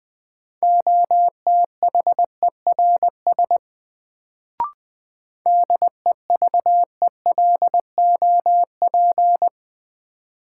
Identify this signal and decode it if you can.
Morse code, heard as a keyed tone, about 20 wpm, 700 Hz. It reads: OTHERS DEVELOP
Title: Sentences from Top 1000 Words - Review Part 2 - New Word Drill - 20wpm